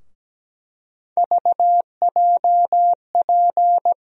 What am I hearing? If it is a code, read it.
Morse code, heard as a keyed tone, about 17 wpm, 700 Hz. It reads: VJP
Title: USA Call Sign Postfixes - 3 Letters 17wpm